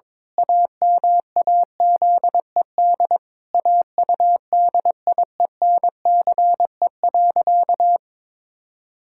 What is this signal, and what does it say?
Morse code, heard as a keyed tone, about 21 wpm, 700 Hz. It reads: AMAZED AUDIENCE.
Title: Pangram Challenge v2 - 22wpm